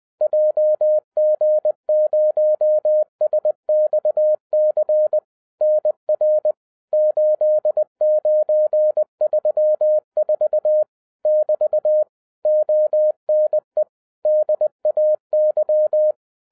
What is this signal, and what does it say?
Morse code, heard as a keyed tone, about 20 wpm, 610 Hz. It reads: JG0SXC NR 8934 <BT> ONE DAY
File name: 2012-01-mp3 / n16-THROUGH EIGHT MONTHS-20wpm